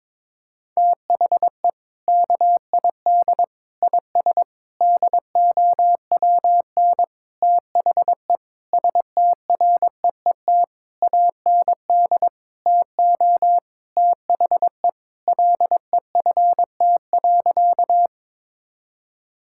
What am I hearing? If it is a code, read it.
Morse code, heard as a keyed tone, about 21 wpm, 700 Hz. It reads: THE KID IS DOWN THE STREET AND TO THE LEFT.
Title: Sentences from Top 1600 Words 22wpm